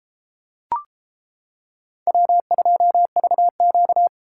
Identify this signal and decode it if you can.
Morse code, heard as a keyed tone, about 31 wpm, 700 Hz. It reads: W2VQ